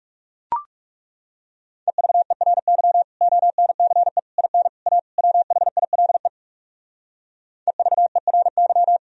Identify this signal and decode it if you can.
Morse code, heard as a keyed tone, about 37 wpm, 700 Hz. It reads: EVERY ONCE IN A WHILE EVERY
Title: Speed-Racing - Sets of 5 Words 30wpm